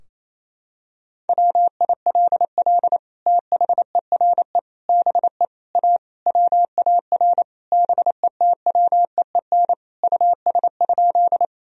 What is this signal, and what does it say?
Morse code, heard as a keyed tone, about 28 wpm, 700 Hz. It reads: WILL THERE BE A WAR BETWEEN US?
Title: Sentences from Top 1000 Words - Review Part 2 - 28wpm